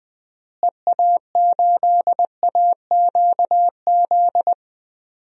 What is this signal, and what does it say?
Morse code, heard as a keyed tone, about 20 wpm, 700 Hz. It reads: EA8AQZ
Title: DXCC-100 Call Signs - 6 Characters - Repeated 2x - 20wpm